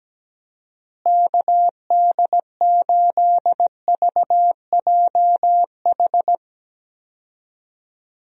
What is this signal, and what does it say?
Morse code, heard as a keyed tone, about 17 wpm, 700 Hz. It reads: KD8VJH
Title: USA Call Signs - 6 Characters - Repeated 2x - 17wpm